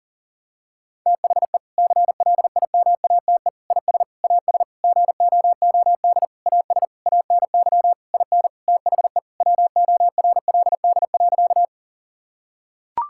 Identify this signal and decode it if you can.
Morse code, heard as a keyed tone, about 40 wpm, 700 Hz. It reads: THE CLIMATE IS AS GOOD AS ANY IN THE WORLD.
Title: Sentences from Top 1600 Words 40wpm